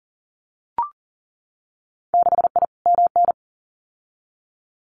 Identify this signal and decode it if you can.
Morse code, heard as a keyed tone, about 40 wpm, 700 Hz. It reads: 6I MN
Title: Winter Field Day Exchanges Encore 40wpm